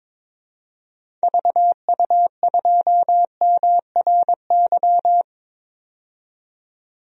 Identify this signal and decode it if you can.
Morse code, heard as a keyed tone, about 23 wpm, 700 Hz. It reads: VU2MRY
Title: DXCC-100 Call Signs - 6 Characters - 22wpm